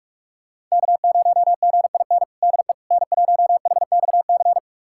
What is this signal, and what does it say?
Morse code, heard as a keyed tone, about 46 wpm, 700 Hz. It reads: K0GIN DE N1HXC
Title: USA Callsign DE Callsign - 5 Characters - 45wpm